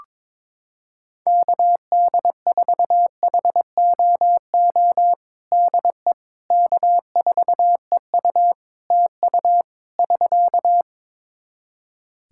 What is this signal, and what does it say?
Morse code, heard as a keyed tone, about 22 wpm, 700 Hz. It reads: KD4HOO DE K4EU TU <SK>
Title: Practice QSO - Exchange 4 - Ending - 22wpm